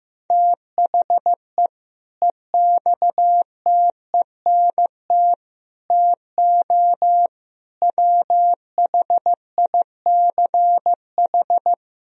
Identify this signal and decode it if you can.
Morse code, heard as a keyed tone, about 15 wpm, 700 Hz. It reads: THE EXTENT TO WHICH
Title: Speed-Racing - Sets of 4 Words 15wpm